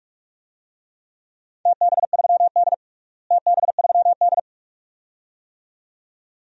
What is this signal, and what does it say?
Morse code, heard as a keyed tone, about 46 wpm, 700 Hz. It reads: TB3D TB3D
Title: DXCC-100 Call Signs - 4 Characters - Repeated 2x - 45wpm